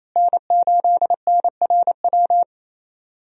Morse code, heard as a keyed tone, about 28 wpm, 700 Hz. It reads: N8NRW
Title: USA Call Signs - 5 Characters - Repeated 2x - 28wpm